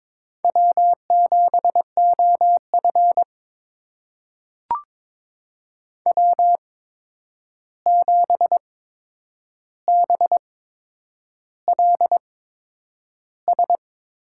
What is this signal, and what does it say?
Morse code, heard as a keyed tone, about 21 wpm, spaced as for 9 wpm, 700 Hz. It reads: W7OF W7BLS
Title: USA Call Signs - 8x Character Spacing - 22wpm